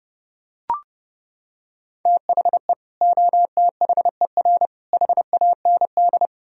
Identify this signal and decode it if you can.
Morse code, heard as a keyed tone, about 29 wpm, 700 Hz. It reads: THE OTHER HAND